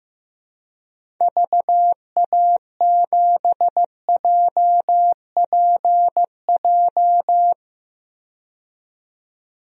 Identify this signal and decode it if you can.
Morse code, heard as a keyed tone, about 15 wpm, 700 Hz. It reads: VA7JPJ
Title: DXCC-100 Call Signs - 6 Characters - 15wpm